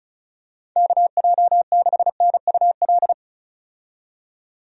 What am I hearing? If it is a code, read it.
Morse code, heard as a keyed tone, about 36 wpm, 700 Hz. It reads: KJ6NUL
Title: USA Call Signs - 6 Characters - 35wpm